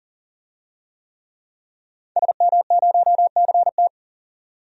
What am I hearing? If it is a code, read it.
Morse code, heard as a keyed tone, about 40 wpm, 700 Hz. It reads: SM0CT
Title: DXCC-100 Call Signs - 5 Characters - 40wpm